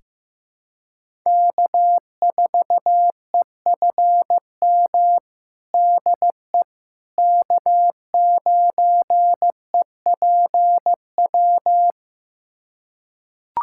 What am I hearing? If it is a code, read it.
Morse code, heard as a keyed tone, about 15 wpm, 700 Hz. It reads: K4EFM DE K9EPW
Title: USA Callsign DE Callsign - 5 Characters - 15wpm